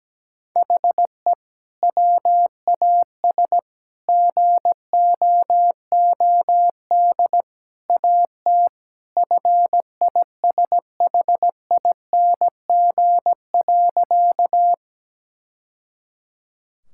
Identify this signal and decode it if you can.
Morse code, heard as a keyed tone, about 17 wpm, 700 Hz. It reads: HE WAS GOOD AT FISHING.
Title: Sentences from Top 1000 Words - Review Part 5 - 17wpm